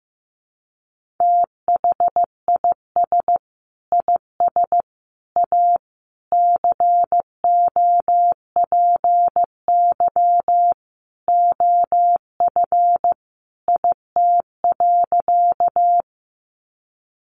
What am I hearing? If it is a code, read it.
Morse code, heard as a keyed tone, about 15 wpm, 700 Hz. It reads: THIS IS A COPY OF IT.
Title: Sentences from Top 1400 Words 15wpm